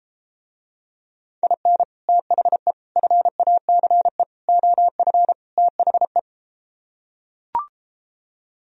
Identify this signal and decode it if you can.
Morse code, heard as a keyed tone, about 33 wpm, 700 Hz. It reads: IN THE FACE OF THE